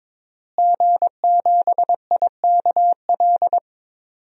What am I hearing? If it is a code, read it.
Morse code, heard as a keyed tone, about 22 wpm, 700 Hz. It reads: G7IKL
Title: DXCC-100 Call Signs - 5 Characters - Repeated 2x - 22wpm